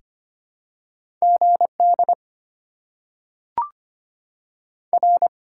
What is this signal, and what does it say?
Morse code, heard as a keyed tone, about 25 wpm, 700 Hz. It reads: G D R
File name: Top 90 QSO Elements - 6x Character Spacing - 25wpm